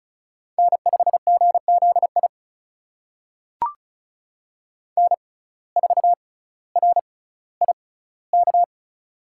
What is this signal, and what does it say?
Morse code, heard as a keyed tone, about 34 wpm, spaced as for 18 wpm, 700 Hz. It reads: N5GZI N4RIK
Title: USA Call Signs - 5 Characters - 6x Character Spacing - 35wpm